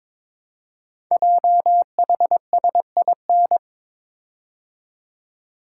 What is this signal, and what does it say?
Morse code, heard as a keyed tone, about 22 wpm, 700 Hz. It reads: JHSIN
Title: Code Groups - 5 Letters - 22wpm